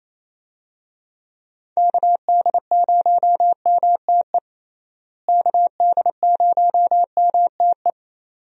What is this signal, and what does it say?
Morse code, heard as a keyed tone, about 27 wpm, 700 Hz. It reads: KD0MTE KD0MTE
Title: USA Call Signs - 6 Characters - Repeated 2x - 28wpm